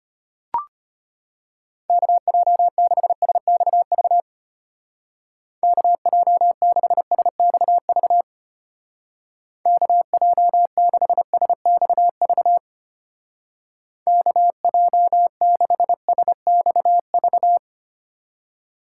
Morse code, heard as a keyed tone, about 31 wpm, 700 Hz. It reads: KJ6SXV KJ6SXV KJ6SXV KJ6SXV
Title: Speed Racing - USA Call Signs 25wpm